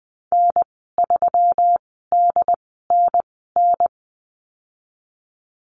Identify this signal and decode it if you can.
Morse code, heard as a keyed tone, about 20 wpm, spaced as for 14 wpm, 700 Hz. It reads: N3DNN
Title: 122 Lesson 20g - Number 3 - Call Signs 20wpm